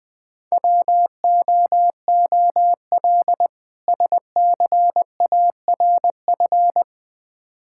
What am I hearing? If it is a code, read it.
Morse code, heard as a keyed tone, about 20 wpm, 700 Hz. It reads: WOOL SCARF
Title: Difficult Words with Hints - 20wpm